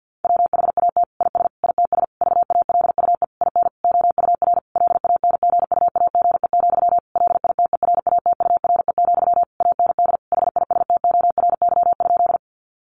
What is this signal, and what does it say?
Morse code, heard as a keyed tone, about 50 wpm, 700 Hz. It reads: WHAT IS ITS VALUE IN OUR LANGUAGE, LITERATURE, AND HISTORY?
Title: Sentences from Top 600 Words 50wpm